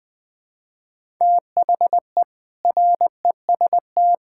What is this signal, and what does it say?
Morse code, heard as a keyed tone, about 20 wpm, 700 Hz. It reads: THE REST
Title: Sentences with 4-Word Phrases - Spoken Beginning - 20wpm